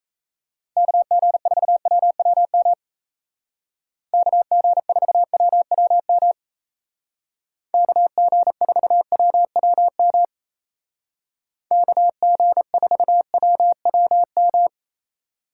Morse code, heard as a keyed tone, about 34 wpm, 700 Hz. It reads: KG4WWM KG4WWM KG4WWM KG4WWM
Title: Speed Racing - USA 2x3 Call Signs 28wpm